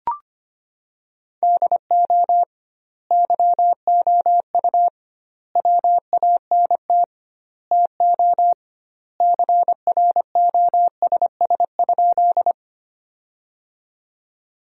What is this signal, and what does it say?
Morse code, heard as a keyed tone, about 25 wpm, 700 Hz. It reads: DO YOU WANT TO CROSS?
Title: 2x Word-Spacing - Sentences from Top 300 Words - 25wpm